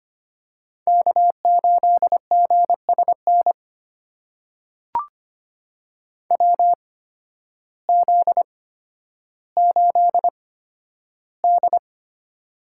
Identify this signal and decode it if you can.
Morse code, heard as a keyed tone, about 25 wpm, spaced as for 12 wpm, 700 Hz. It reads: K8GSN WZ8D
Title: USA Call Signs - 8x Character Spacing - 25wpm